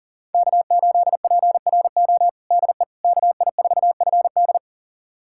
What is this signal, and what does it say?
Morse code, heard as a keyed tone, about 40 wpm, 700 Hz. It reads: K8PRO DE KI4FD
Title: USA Callsign DE Callsign - 5 Characters - 40wpm